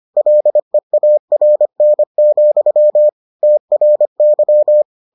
Morse code, heard as a keyed tone, about 25 wpm, 600 Hz. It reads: LEARN, TRY